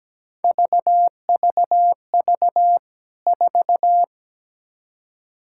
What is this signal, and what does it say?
Morse code, heard as a keyed tone, about 17 wpm, 700 Hz. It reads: VVV 4
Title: K1USN SST QSO Practice 17wpm